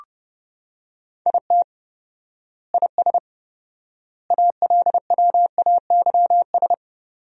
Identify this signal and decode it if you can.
Morse code, heard as a keyed tone, about 30 wpm, 700 Hz. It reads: IT IS ALWAYS
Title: 4x Word-Spacing - Sentences from Top 300 Words - 30wpm